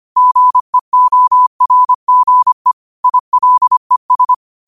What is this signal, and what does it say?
Morse code, heard as a keyed tone, about 25 wpm, 1000 Hz. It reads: GEORGE ILES
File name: qotdfiles / 25wpm